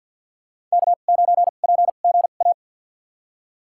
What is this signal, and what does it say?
Morse code, heard as a keyed tone, about 53 wpm, spaced as for 37 wpm, 700 Hz. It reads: K9PGA
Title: USA Call Signs - 5 Characters - 2x Character Spacing - 50wpm